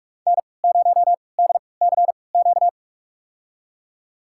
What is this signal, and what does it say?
Morse code, heard as a keyed tone, about 45 wpm, spaced as for 25 wpm, 700 Hz. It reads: N0DCQ
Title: USA Call Signs - 5 Characters - 3x Character Spacing - 45wpm